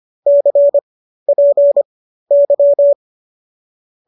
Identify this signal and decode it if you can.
Morse code, heard as a keyed tone, about 25 wpm, spaced as for 13 wpm, 590 Hz. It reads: CPY
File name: qso207 13